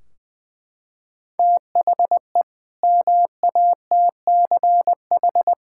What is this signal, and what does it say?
Morse code, heard as a keyed tone, about 20 wpm, 700 Hz. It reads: THE MATCH
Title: Sentences from Top 1600 Words 20wpm